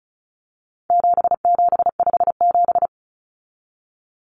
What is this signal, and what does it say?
Morse code, heard as a keyed tone, about 34 wpm, 700 Hz. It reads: 7757
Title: Numbers - 4-Digit 35wpm